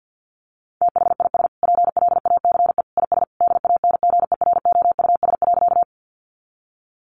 Morse code, heard as a keyed tone, about 50 wpm, 700 Hz. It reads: THIS PLACE IS DANGEROUS.
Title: Sentences from Top 1200 Words 50wpm